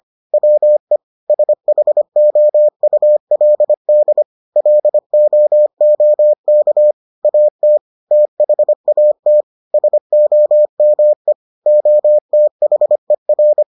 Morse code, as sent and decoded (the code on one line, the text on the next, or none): .-- . / ... .... --- ..- .-.. -.. / .-.. --- --- -.- / .- - / - .... .- - / ... --- -- . / --- - .... .-.
WE SHOULD LOOK AT THAT SOME OTHER